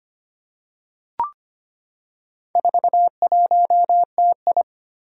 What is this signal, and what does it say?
Morse code, heard as a keyed tone, about 25 wpm, 700 Hz. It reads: 41TI
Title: Four Challenging Letter-Number Combinations 25wpm